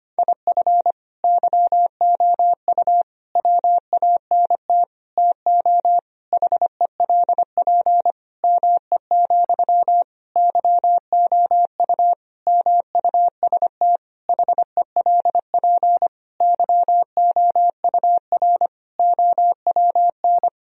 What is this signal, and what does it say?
Morse code, heard as a keyed tone, about 25 wpm, 700 Hz. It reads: IF YOU WANT TO HELP ME, YOU MUST HELP YOUR OWN
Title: Sentences from Top 1000 Words - Review Part 1 - 25wpm